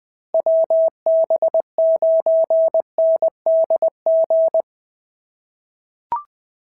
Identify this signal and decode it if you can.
Morse code, heard as a keyed tone, about 19 wpm, 650 Hz. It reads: WB9NDG
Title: POTA QSO Practice - 20wpm